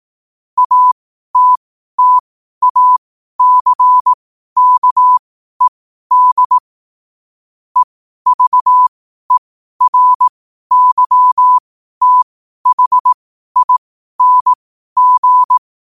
Morse code, heard as a keyed tone, about 18 wpm, spaced as for 12 wpm, 1000 Hz. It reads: ATTACKED EVERYTHING
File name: qotdfiles / 12wpm